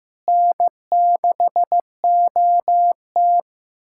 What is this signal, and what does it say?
Morse code, heard as a keyed tone, about 15 wpm, 700 Hz. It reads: N6OT
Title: USA Callsign DE Callsign - 4 Characters - 15wpm